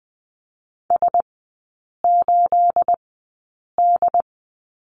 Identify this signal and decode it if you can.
Morse code, heard as a keyed tone, about 20 wpm, spaced as for 8 wpm, 700 Hz. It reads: S8D
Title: 212 Lesson 34e - Number 8 - Three Characters 20wpm